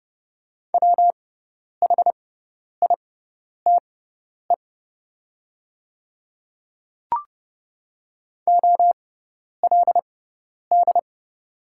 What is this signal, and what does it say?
Morse code, heard as a keyed tone, about 29 wpm, spaced as for 10 wpm, 690 Hz. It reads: WHITE OLD
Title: 6x Character Spacing - Top 300 Words - 30wpm